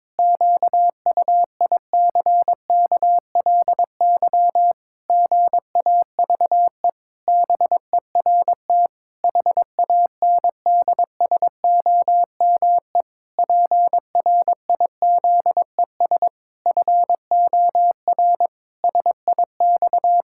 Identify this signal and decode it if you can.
Morse code, heard as a keyed tone, about 22 wpm, 700 Hz. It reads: QUICKLY GAVE BERT HANDSOME PRIZES FOR SIX
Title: Pangram Challenge v2 - 22wpm